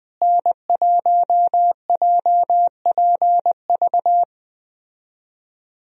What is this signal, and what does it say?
Morse code, heard as a keyed tone, about 20 wpm, 700 Hz. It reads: N1JPV
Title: USA Call Signs - 5 Characters - Repeated 3x - 20wpm